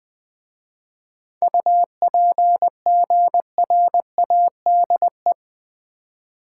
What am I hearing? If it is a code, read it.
Morse code, heard as a keyed tone, about 20 wpm, 700 Hz. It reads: UPGRADE
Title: 7 Letter Words - First 3 Letters Given - 20wpm